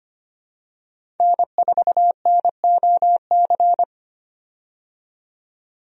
Morse code, heard as a keyed tone, about 25 wpm, 690 Hz. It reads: N4NOC
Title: USA Call Signs - 5 Characters - 25wpm